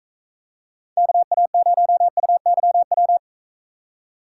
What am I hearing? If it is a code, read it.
Morse code, heard as a keyed tone, about 40 wpm, 700 Hz. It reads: KA0UYW